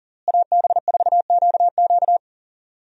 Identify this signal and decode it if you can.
Morse code, heard as a keyed tone, about 40 wpm, 700 Hz. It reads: AB4QQ
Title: USA Call Signs - 5 Characters - Repeated 2x - 40wpm